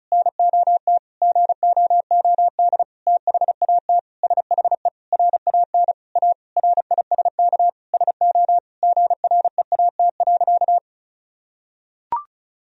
Morse code, heard as a keyed tone, about 36 wpm, 700 Hz. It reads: NOT GOOD THAT SHE RAN A RISK SO GREAT.
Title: Sentences from Top 1000 Words - Review Part 4 - 35wpm